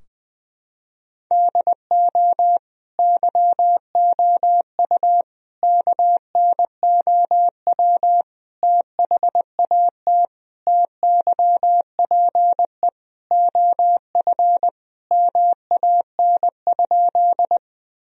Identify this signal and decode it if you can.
Morse code, heard as a keyed tone, about 21 wpm, 700 Hz. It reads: DO YOU KNOW THAT TYPE OF MAN?
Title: Sentences from Top 1000 Words - Review Part 3 - 20wpm